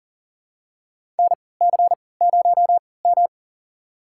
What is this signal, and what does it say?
Morse code, heard as a keyed tone, about 40 wpm, spaced as for 23 wpm, 700 Hz. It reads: NC0M
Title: USA Call Signs - 4 Characters - 3x Character Spacing - 40wpm